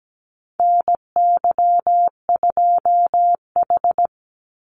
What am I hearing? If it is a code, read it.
Morse code, heard as a keyed tone, about 17 wpm, 700 Hz. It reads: NY2H
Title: Call Signs Repeated 2x - 17wpm